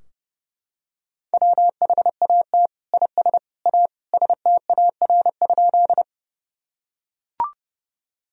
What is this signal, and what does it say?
Morse code, heard as a keyed tone, about 30 wpm, 700 Hz. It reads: WHAT IS A STAR?
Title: Sentences from Top 1000 Words - Review Part 2 - 30wpm